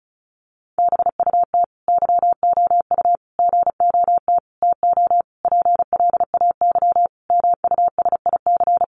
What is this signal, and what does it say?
Morse code, heard as a keyed tone, about 34 wpm, 700 Hz. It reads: BUT YOU GOT TO PLAY MUSIC
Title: Sentences from Top 300 Words 35wpm